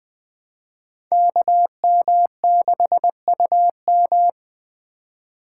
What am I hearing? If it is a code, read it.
Morse code, heard as a keyed tone, about 21 wpm, 700 Hz. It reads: KM6UM